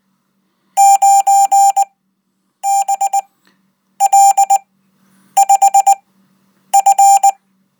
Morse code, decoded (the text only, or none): 9BL5F